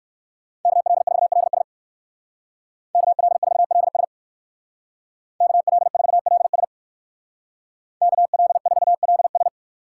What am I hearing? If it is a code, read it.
Morse code, heard as a keyed tone, about 55 wpm, 700 Hz. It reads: KL4LS KL4LS KL4LS KL4LS
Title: Speed Racing - USA 2x2 Call Signs 45wpm